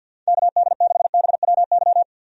Morse code, heard as a keyed tone, about 51 wpm, 700 Hz. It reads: KD6BWY
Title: USA Call Signs - 6 Characters - 50wpm